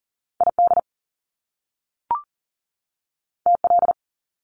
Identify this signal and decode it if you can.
Morse code, heard as a keyed tone, about 40 wpm, 700 Hz. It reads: ID TL
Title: Top Two Letter Combinations in Words 40wpm